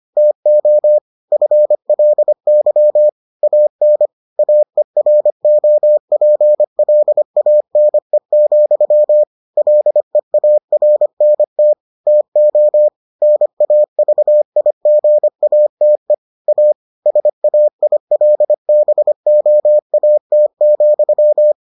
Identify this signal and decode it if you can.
Morse code, heard as a keyed tone, about 25 wpm, 600 Hz. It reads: TO FLY AN AEROPLANE, LEARNT TO NAVIGATE A SAILBOAT,